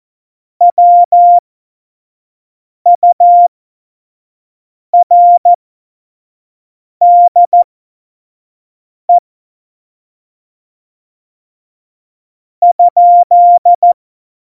Morse code, heard as a keyed tone, about 14 wpm, spaced as for 5 wpm, 700 Hz. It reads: WURDE ?
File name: Morsekurs / Witze / Witz 13 March 2018, 25 F70